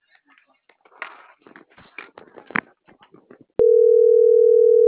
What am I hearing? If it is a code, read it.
Ringback tone (North America), 440 + 480 Hz.